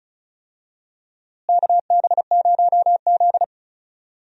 Morse code, heard as a keyed tone, about 34 wpm, 700 Hz. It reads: KB0Z
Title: USA Call Signs - 4 Characters - Repeated 2x - 35wpm